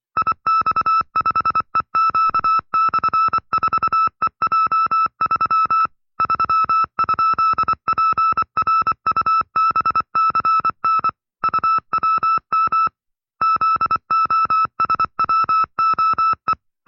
Morse code, heard as a keyed tone, about 24 wpm, 1310 Hz. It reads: IX5EQ/4EJ3 3?PRUBCN UWM ZOSWOE
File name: FreeFiles / CW-practice / 20WPM-side2